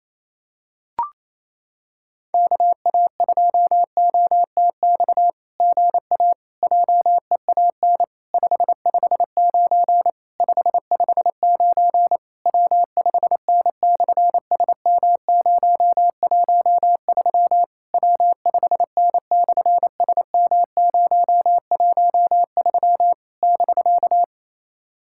Morse code, heard as a keyed tone, about 28 wpm, 700 Hz. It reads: KA2OTX GA JEAN 559 559 W5N/SM013 W5N/SM013 <BK>
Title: SOTA QSO Practice 28wpm